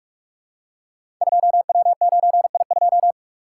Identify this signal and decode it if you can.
Morse code, heard as a keyed tone, about 44 wpm, 700 Hz. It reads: 2W9I2
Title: Code Groups - 5 Letter:Numbers - 45wpm